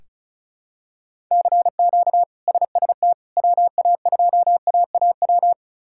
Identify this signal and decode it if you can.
Morse code, heard as a keyed tone, about 36 wpm, 700 Hz. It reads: CQ SST WA2AAW
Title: K1USN SST QSO Practice - 35wpm